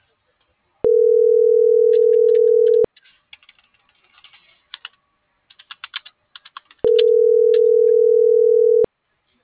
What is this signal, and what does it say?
Ringback tone (North America), 440 + 480 Hz.